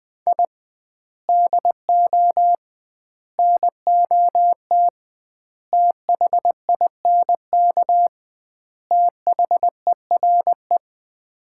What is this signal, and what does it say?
Morse code, heard as a keyed tone, about 21 wpm, 700 Hz. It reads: I DO NOT THINK THERE
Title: 2x Word-Spacing - Sets of 5 Words - 20wpm